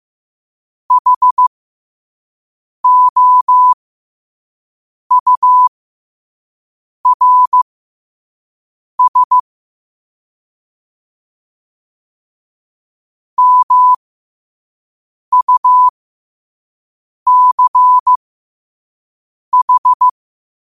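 Morse code, heard as a keyed tone, about 15 wpm, spaced as for 5 wpm, 1000 Hz. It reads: HOURS MUCH
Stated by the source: Quote-of-the-day Morse code practice at 5 wpm.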